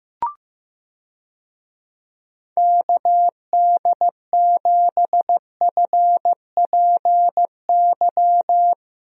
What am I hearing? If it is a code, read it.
Morse code, heard as a keyed tone, about 15 wpm, 700 Hz. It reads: KD7FPY
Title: USA Call Signs - 6 Characters - Repeated 3x - 15wpm